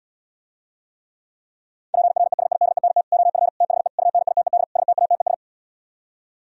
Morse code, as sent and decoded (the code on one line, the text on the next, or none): .--. .-. ... .. -.. -. - / --- ..-. / - .... / ..- -. .. - . -.. / ... - .- - . ...
PRESIDENT OF THE UNITED STATES